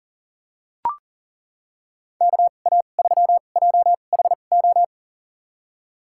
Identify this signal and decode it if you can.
Morse code, heard as a keyed tone, about 39 wpm, spaced as for 29 wpm, 700 Hz. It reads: KA3JHO
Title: USA Call Signs - 6 Characters - 2x Character Spacing - 40wpm